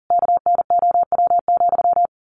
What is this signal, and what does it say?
Morse code, heard as a keyed tone, about 37 wpm, 700 Hz. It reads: KNOW,
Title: A Sound Of Thunder 40wpm